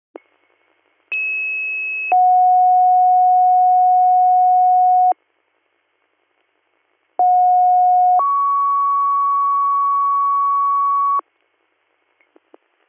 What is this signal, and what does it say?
Two-tone page, heard as a one-second tone, then a three-second tone, about 2600 then 730 Hz, about 730 then 1120 Hz. It is heard from 1.1 to 11.2 s.